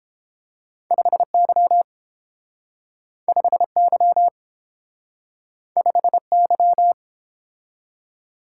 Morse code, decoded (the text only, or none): 5Y 5Y 5Y